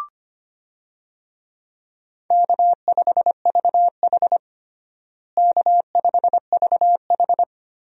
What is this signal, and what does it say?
Morse code, heard as a keyed tone, about 25 wpm, 700 Hz. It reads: K5VH K5VH
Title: USA Call Signs - 4 Characters - Repeated 3x - 25wpm